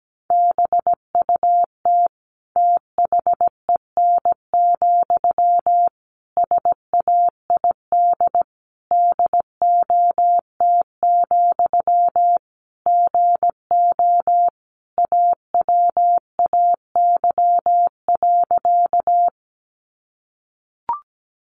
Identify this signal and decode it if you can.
Morse code, heard as a keyed tone, about 17 wpm, 700 Hz. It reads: BUT THEN, SAID DOT, GO AWAY.
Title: Sentences from Top 1500 Words 17wpm